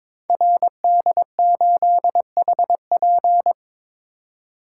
Morse code, heard as a keyed tone, about 22 wpm, 700 Hz. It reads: RD8HP
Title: Code Groups - 5 Letter:Numbers - 22wpm